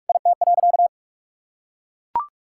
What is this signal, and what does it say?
Morse code, heard as a keyed tone, about 43 wpm, 700 Hz. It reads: IT.
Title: Sentences from Top 1600 Words 45wpm